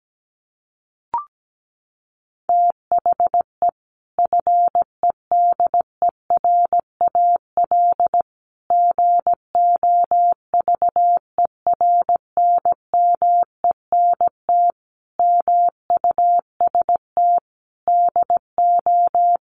Morse code, heard as a keyed tone, about 17 wpm, 700 Hz. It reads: THE FEDERAL GOVERNMENT MUST DO
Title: Sentences from Top 1100 Words 17wpm